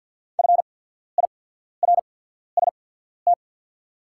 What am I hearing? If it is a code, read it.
Morse code, heard as a keyed tone, about 48 wpm, spaced as for 14 wpm, 690 Hz. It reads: FIRST
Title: 8x Character Spacing - Top 200 Words - 50wpm